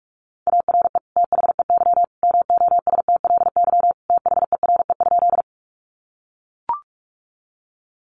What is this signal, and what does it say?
Morse code, heard as a keyed tone, about 44 wpm, 700 Hz. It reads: ARE THEY MOSTLY THERE?
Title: Sentences from Top 1300 Words 45wpm